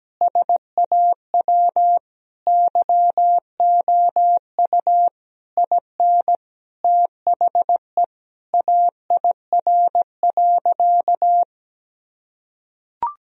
Morse code, heard as a keyed tone, about 17 wpm, 700 Hz. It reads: SAW YOU IN THE AIR.
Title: Sentences from Top 1000 Words - Review Part 1 - 17wpm